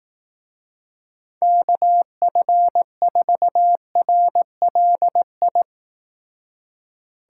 Morse code, heard as a keyed tone, about 18 wpm, 700 Hz. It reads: KF4RLI
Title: Speed Racing - USA Call Signs 15wpm